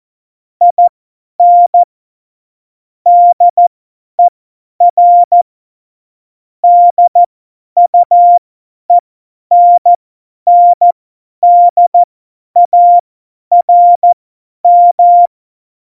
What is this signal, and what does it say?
Morse code, heard as a keyed tone, about 14 wpm, spaced as for 10 wpm, 700 Hz. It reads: IN DER DUENNDARM
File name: Morsekurs / Witze / Witz 25. September 2018, 50 F70